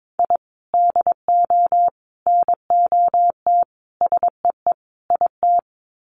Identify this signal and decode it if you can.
Morse code, heard as a keyed tone, about 22 wpm, 700 Hz. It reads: I DO NOT SEE IT
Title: Sets of 5 Words Encore2 22wpm